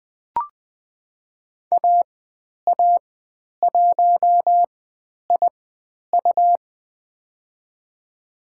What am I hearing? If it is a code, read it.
Morse code, heard as a keyed tone, about 20 wpm, spaced as for 10 wpm, 700 Hz. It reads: AA1IU
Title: 109 Lesson 18g - Letter m - Call Signs 20wpm